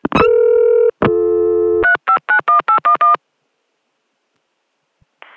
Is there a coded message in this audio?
Yes: DTMF from 1.8 to 3.2 s, heard as telephone touch tones, tones of about 110 ms. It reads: 3591811